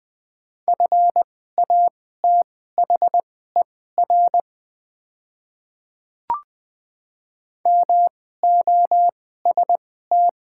Morse code, heard as a keyed tone, about 20 wpm, spaced as for 14 wpm, 700 Hz. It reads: FATHER MOST
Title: 2x Character Spacing - Top 200 Words - 20wpm